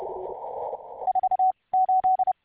Morse code, heard as a keyed tone, about 30 wpm, 750 Hz, with radio noise. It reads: 4 8